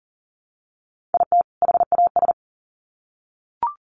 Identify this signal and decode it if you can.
Morse code, heard as a keyed tone, about 40 wpm, 700 Hz. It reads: IT HAS
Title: Sets of 2 Words 40wpm